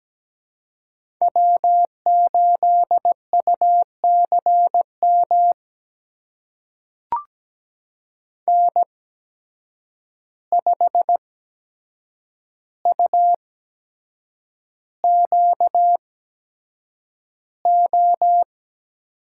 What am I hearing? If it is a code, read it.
Morse code, heard as a keyed tone, about 17 wpm, spaced as for 7 wpm, 700 Hz. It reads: W8UCM N5UQO
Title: USA Call Signs - 5 Characters - 8x Character Spacing - 17wpm